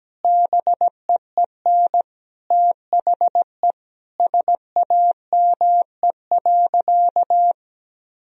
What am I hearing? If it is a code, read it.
Morse code, heard as a keyed tone, about 17 wpm, 700 Hz. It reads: BEEN THE SAME.
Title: Sentences from Top 1000 Words - Review Part 1 - 17wpm